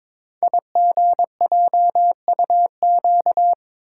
Morse code, heard as a keyed tone, about 22 wpm, 700 Hz. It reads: IGJUQ
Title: Code Groups - 5 Letters - 22wpm